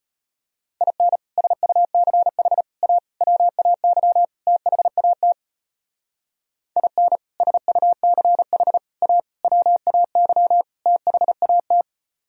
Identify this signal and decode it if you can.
Morse code, heard as a keyed tone, about 32 wpm, 700 Hz. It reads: IN SUCH A WAY THAT IN SUCH A WAY THAT